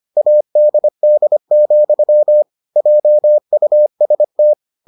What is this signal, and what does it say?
Morse code, heard as a keyed tone, about 25 wpm, 600 Hz. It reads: ADD, JUST